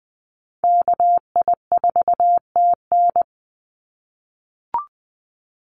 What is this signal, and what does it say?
Morse code, heard as a keyed tone, about 20 wpm, 700 Hz. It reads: KI4TN
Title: NA SOTA Chaser Callsigns 20wpm